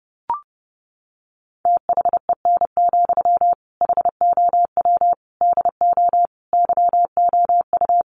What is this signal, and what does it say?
Morse code, heard as a keyed tone, about 30 wpm, 700 Hz. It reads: THEN, HOW DO YOU
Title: Sentences from Top 100 Words 30wpm